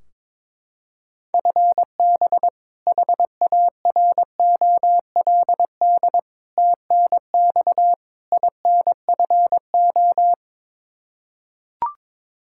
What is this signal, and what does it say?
Morse code, heard as a keyed tone, about 22 wpm, 700 Hz. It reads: FB HAROLD TNX INFO
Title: Practice QSO - Exchange 3 - More Info - 22wpm